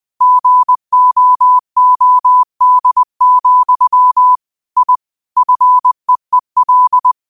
Morse code, heard as a keyed tone, about 20 wpm, 1000 Hz. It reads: GOOD, I FEEL
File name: qotdfiles / 20wpm